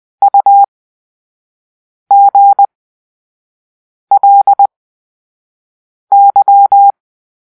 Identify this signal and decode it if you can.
Morse code, heard as a keyed tone, about 20 wpm, spaced as for 5 wpm, 800 Hz. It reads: UGLY
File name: Grp-3 Drill-4